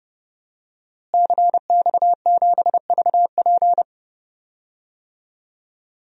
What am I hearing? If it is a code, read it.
Morse code, heard as a keyed tone, about 32 wpm, 700 Hz. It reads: CX7VP
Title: DXCC-100 Call Signs - 5 Characters - Repeated 2x - 30wpm